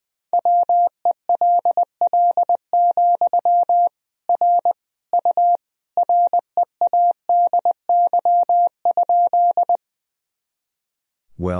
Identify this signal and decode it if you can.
Morse code, heard as a keyed tone, about 20 wpm, 700 Hz. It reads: WELL, R U READY?
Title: Sentences from Top 400 Words and Abbreviations 20wpm